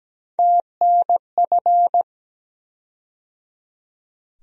Morse code, heard as a keyed tone, about 17 wpm, 700 Hz. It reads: TNF
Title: USA Call Sign Postfixes - 3 Letters 17wpm